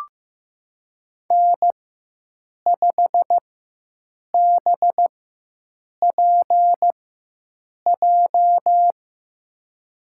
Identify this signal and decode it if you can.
Morse code, heard as a keyed tone, about 16 wpm, spaced as for 7 wpm, 700 Hz. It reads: N5BPJ
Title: USA Call Signs - 5 Characters - 4x Character Spacing - 15wpm